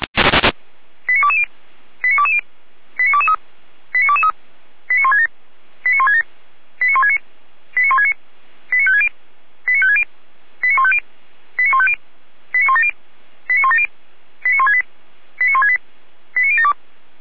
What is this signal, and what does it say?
Five-tone selective call (ZVEI), heard as quick runs of five tones, tones of about 70 ms. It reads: 89220 89220 89223 89223 89167 89167 89169 89169 89560 89560 89150 89150 89170 89170 89168 89168 89072